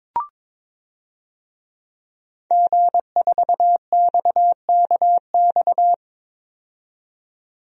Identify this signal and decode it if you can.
Morse code, heard as a keyed tone, about 22 wpm, 700 Hz. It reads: G4XKX
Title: DXCC-100 Call Signs - 5 Characters - 22wpm